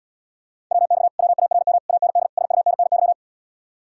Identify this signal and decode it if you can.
Morse code, heard as a keyed tone, about 77 wpm, 700 Hz. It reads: WX WARM AND SUNNY